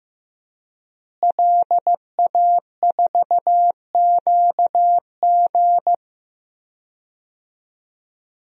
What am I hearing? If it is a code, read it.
Morse code, heard as a keyed tone, about 15 wpm, 700 Hz. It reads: LA4QG
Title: DXCC-100 Call Signs - 5 Characters - Repeated 2x - 15wpm